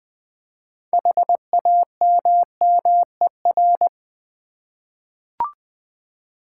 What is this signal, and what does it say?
Morse code, heard as a keyed tone, about 20 wpm, 700 Hz. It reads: HAMMER
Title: Easy Words with Hints - 20wpm